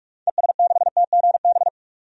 Morse code, heard as a keyed tone, about 47 wpm, 700 Hz. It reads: ES6TGB
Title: DXCC-100 Call Signs - 6 Characters - 45wpm